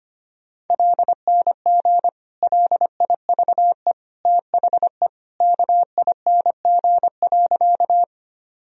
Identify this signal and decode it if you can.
Morse code, heard as a keyed tone, about 25 wpm, 700 Hz. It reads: LNG LIVE THE KING.
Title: Sentences from Top 400 Words and Abbreviations 25wpm